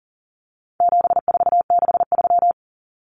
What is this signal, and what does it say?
Morse code, heard as a keyed tone, about 40 wpm, 700 Hz. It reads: 7463
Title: Common Cut Numbers - 4-Digit 40wpm